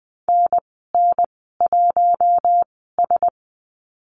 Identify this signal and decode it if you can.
Morse code, heard as a keyed tone, about 20 wpm, spaced as for 15 wpm, 700 Hz. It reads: NN1S
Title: 042 Lesson 8g - Number 1 - Call Signs 20wpm